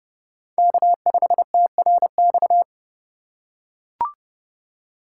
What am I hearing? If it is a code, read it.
Morse code, heard as a keyed tone, about 32 wpm, 700 Hz. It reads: K5TRX
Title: USA Call Signs - 5 Characters - 4x Character Spacing - 30wpm